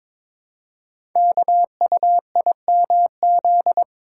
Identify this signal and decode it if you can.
Morse code, heard as a keyed tone, about 22 wpm, 700 Hz. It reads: KUIMZ
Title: Code Groups - 5 Letters - 22wpm